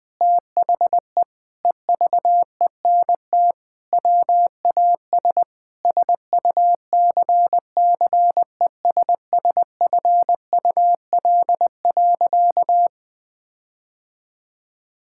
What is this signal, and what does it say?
Morse code, heard as a keyed tone, about 21 wpm, 700 Hz. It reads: THE EVENT WAS SUCCESSFUL.
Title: Sentences with 7-Word Phrases - 20wpm